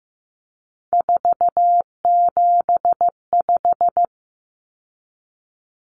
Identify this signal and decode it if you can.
Morse code, heard as a keyed tone, about 15 wpm, 700 Hz. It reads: 475
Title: Common Cut Numbers - 3-Digit 15wpm